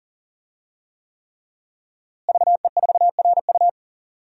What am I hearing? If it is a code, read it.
Morse code, heard as a keyed tone, about 40 wpm, 700 Hz. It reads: VE4RU
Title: DXCC-100 Call Signs - 5 Characters - Repeated 3x - 40wpm